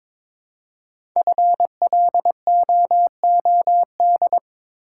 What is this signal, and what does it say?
Morse code, heard as a keyed tone, about 22 wpm, 700 Hz. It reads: FLOOD
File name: 5 Letter Words - First 3 Letters Given - 22wpm